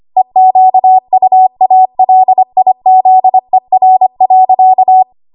Morse code, heard as a keyed tone, about 25 wpm, 750 Hz. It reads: EQUALIZER.
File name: Morse / 25WPM